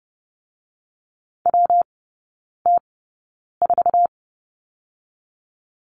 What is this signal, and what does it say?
Morse code, heard as a keyed tone, about 30 wpm, 700 Hz. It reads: W T 4
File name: Lesson 19e - Letter w - Three Characters 30wpm